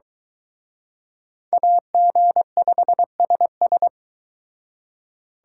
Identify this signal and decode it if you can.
Morse code, heard as a keyed tone, about 23 wpm, 700 Hz. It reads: AG5SS